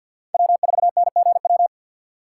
Morse code, heard as a keyed tone, about 51 wpm, 700 Hz. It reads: W4NGW
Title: USA Call Signs - 4x Character Spacing - 50wpm